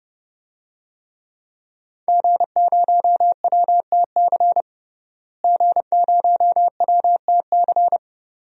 Morse code, heard as a keyed tone, about 30 wpm, 700 Hz. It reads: G0WTC G0WTC
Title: DXCC-100 Call Signs - 5 Characters - Repeated 2x - 30wpm